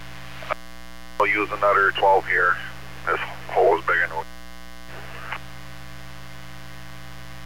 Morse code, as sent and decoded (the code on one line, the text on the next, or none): none